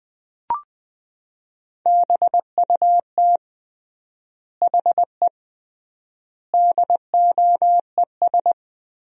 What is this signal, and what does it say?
Morse code, heard as a keyed tone, about 20 wpm, 700 Hz. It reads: BUT HE DOES
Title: 3x Word-Spacing - Sentences from Top 200 Words - 20wpm